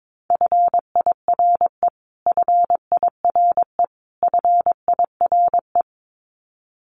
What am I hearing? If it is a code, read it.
Morse code, heard as a keyed tone, about 22 wpm, 700 Hz. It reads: FIRE FIRE FIRE
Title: Top 500 Words Repeated 22wpm